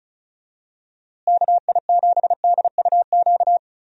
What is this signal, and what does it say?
Morse code, heard as a keyed tone, about 34 wpm, 700 Hz. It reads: KI7DUQ
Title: USA Call Signs - 6 Characters - Repeated 3x - 35wpm